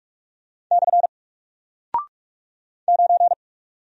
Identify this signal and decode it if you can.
Morse code, heard as a keyed tone, about 44 wpm, 700 Hz. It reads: / 9
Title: Single Character (Letter:Number:Punctuation:Prosign) Alt 45wpm